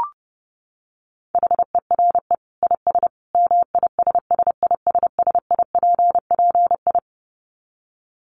Morse code, heard as a keyed tone, about 30 wpm, 700 Hz. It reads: HERE IS MISSISSIPPI
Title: Dit Challenge 30wpm